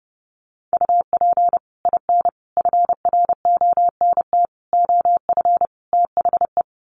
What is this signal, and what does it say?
Morse code, heard as a keyed tone, about 28 wpm, 700 Hz. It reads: UP IN FRONT OF THE
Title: Sets of 5 Words Encore2 30wpm